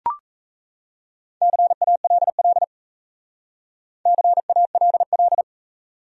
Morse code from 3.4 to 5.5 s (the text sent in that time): CALL